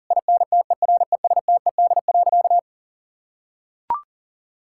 Morse code, heard as a keyed tone, about 40 wpm, 700 Hz. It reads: INTERESTED.
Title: Sentences from Top 1000 Words - Review Part 6 - 40wpm